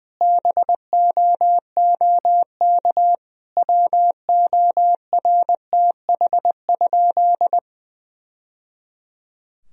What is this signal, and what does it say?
Morse code, heard as a keyed tone, about 20 wpm, 700 Hz. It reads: BOOK WORTH?
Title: Sentences from Top 1000 Words - Review Part 1 - 20wpm